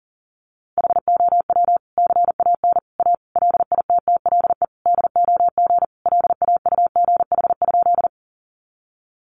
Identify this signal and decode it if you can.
Morse code, heard as a keyed tone, about 40 wpm, 700 Hz. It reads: HOW CAN A LITTLE DOG LAUGH?
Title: Sentences from Top 500 Words 40wpm